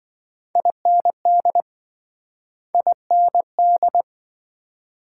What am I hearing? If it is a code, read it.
Morse code, heard as a keyed tone, about 28 wpm, 700 Hz. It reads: IND IND